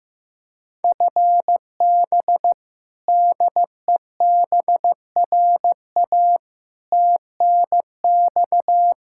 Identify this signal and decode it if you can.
Morse code, heard as a keyed tone, about 15 wpm, 700 Hz. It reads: FB DEBRA TNX
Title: Practice QSO - Exchange 3 - More Info - 15wpm